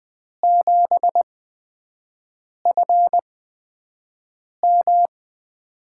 Morse code, heard as a keyed tone, about 20 wpm, spaced as for 5 wpm, 700 Hz. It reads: 7FM
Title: USA Call Signs - 8x Character Spacing - 20wpm